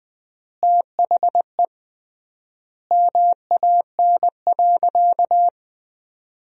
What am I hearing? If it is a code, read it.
Morse code, heard as a keyed tone, about 21 wpm, 700 Hz. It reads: THE MAN.
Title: 3x Word-Spacing - Sentences from Top 400 Words - 20wpm